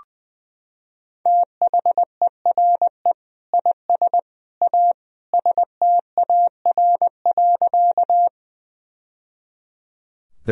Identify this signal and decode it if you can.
Morse code, heard as a keyed tone, about 20 wpm, 700 Hz. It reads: THERE IS A STAR.
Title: Sentences from Top 1000 Words - Review Part 2 - 20wpm